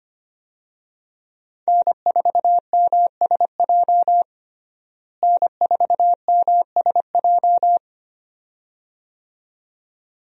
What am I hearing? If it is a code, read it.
Morse code, heard as a keyed tone, about 25 wpm, 700 Hz. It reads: N4MSJ N4MSJ
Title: USA Call Signs - 5 Characters - Repeated 2x - 25wpm